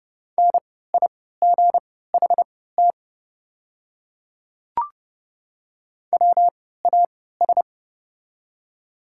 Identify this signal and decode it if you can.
Morse code, heard as a keyed tone, about 30 wpm, spaced as for 17 wpm, 690 Hz. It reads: NIGHT WAS
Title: 3x Character Spacing - Top 300 Words - 30wpm